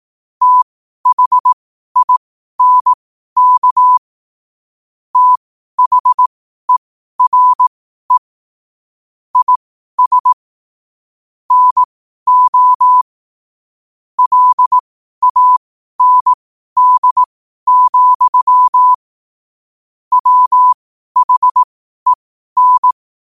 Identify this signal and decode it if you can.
Morse code, heard as a keyed tone, about 18 wpm, spaced as for 12 wpm, 1000 Hz. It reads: THINK THERE IS NO LAND, WHEN
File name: qotdfiles / 12wpm